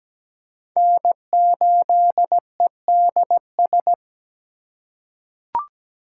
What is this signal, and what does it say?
Morse code, heard as a keyed tone, about 17 wpm, 700 Hz. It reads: N8EDS
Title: USA Call Signs - 5 Characters - 3x Character Spacing - 17wpm